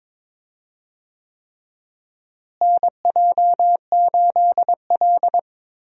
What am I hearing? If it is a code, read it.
Morse code, heard as a keyed tone, about 21 wpm, 700 Hz. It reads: NJ8L
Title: USA Call Signs - 4 Characters - Repeated 3x - 22wpm